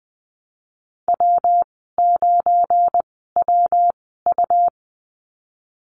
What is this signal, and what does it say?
Morse code, heard as a keyed tone, about 20 wpm, spaced as for 14 wpm, 700 Hz. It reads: W9WU
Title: 231 Lesson 37g - Letter x - Call Signs 20wpm